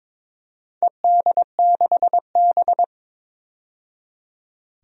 Morse code, heard as a keyed tone, about 22 wpm, 700 Hz. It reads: ED6B
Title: DXCC-100 Call Signs - 4 Characters - 22wpm